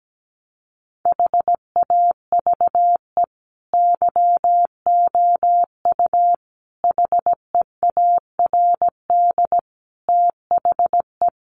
Morse code, heard as a keyed tone, about 17 wpm, 700 Hz. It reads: HAVE YOU HEARD THE
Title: Sentences from Top 1400 Words 17wpm